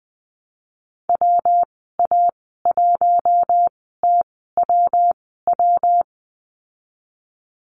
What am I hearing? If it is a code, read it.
Morse code, heard as a keyed tone, about 19 wpm, spaced as for 14 wpm, 700 Hz. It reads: WA1TWW
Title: 147 Lesson 24g - Letter y - Call Signs 20wpm